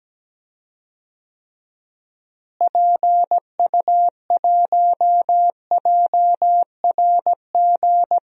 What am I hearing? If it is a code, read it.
Morse code, heard as a keyed tone, about 17 wpm, 700 Hz. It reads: PU1JRG
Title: DXCC-100 Call Signs - 6 Characters - Repeated 2x - 17wpm